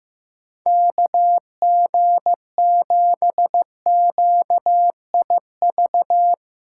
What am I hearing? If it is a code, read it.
Morse code, heard as a keyed tone, about 16 wpm, 700 Hz. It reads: KG7QIV
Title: Speed Racing - USA 2x3 Call Signs 15wpm